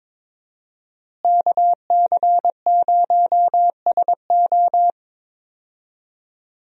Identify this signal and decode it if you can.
Morse code, heard as a keyed tone, about 22 wpm, 700 Hz. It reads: KC0SO
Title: USA Call Signs - 5 Characters - Repeated 2x - 22wpm